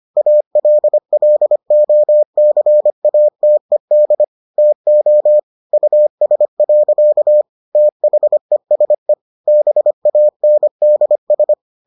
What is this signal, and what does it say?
Morse code, heard as a keyed tone, about 25 wpm, 600 Hz. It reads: ALLOCATED TO US. THESE BANDS